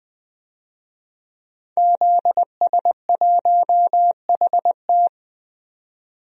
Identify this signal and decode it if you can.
Morse code, heard as a keyed tone, about 20 wpm, 700 Hz. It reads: ZS1HT
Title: DXCC-100 Call Signs - 5 Characters - 20wpm